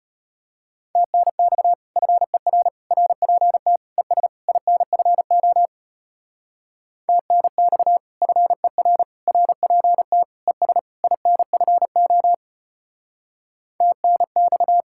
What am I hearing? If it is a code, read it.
Morse code, heard as a keyed tone, about 33 wpm, 700 Hz. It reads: TNX FER RPT ES INFO TNX FER RPT ES INFO TNX